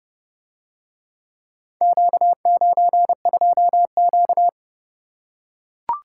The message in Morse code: Q92Q

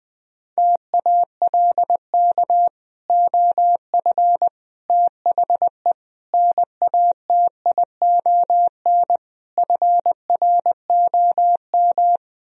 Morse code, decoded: TALK OF THE NATION FROM

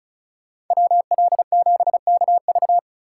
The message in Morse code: WL7KV

